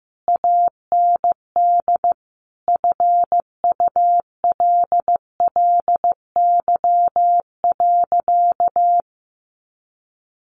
Morse code, heard as a keyed tone, about 15 wpm, 700 Hz. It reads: AND FULLY.
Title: Sentences from Top 1300 Words 15wpm